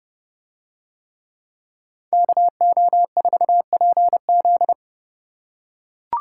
Morse code, heard as a keyed tone, about 27 wpm, 700 Hz. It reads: KO4PZ